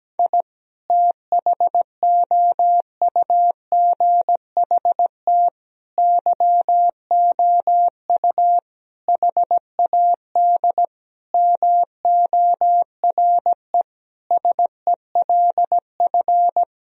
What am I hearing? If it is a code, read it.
Morse code, heard as a keyed tone, about 17 wpm, 700 Hz. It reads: I THOUGHT YOU HAD MORE SELF
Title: Sentences from Top 1000 Words - Review Part 3 - 17wpm